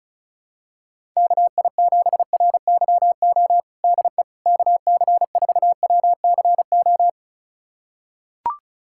Morse code, heard as a keyed tone, about 34 wpm, 700 Hz. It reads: KI7RYO DE KC4WCO
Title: USA Callsign DE Callsign - 6 Characters - 35wpm